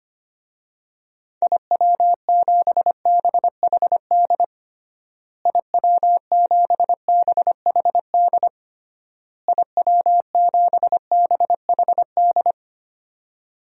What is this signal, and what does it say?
Morse code, heard as a keyed tone, about 25 wpm, 700 Hz. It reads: IW7BHD IW7BHD IW7BHD
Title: DXCC-100 Call Signs - 6 Characters - Repeated 3x - 25wpm